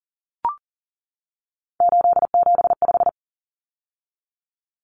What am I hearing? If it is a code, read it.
Morse code, heard as a keyed tone, about 40 wpm, 700 Hz. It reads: 875
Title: Numbers - 3-Digit 40wpm